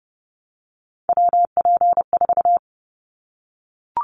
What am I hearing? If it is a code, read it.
Morse code, heard as a keyed tone, about 29 wpm, 700 Hz. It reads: WP4
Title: Callsign Prefixes for Top 25 DXCC - 30wpm